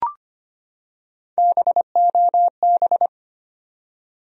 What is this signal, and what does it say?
Morse code, heard as a keyed tone, about 25 wpm, 700 Hz. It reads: BOB
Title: USA Common Names - Max 4 Letters - 25wpm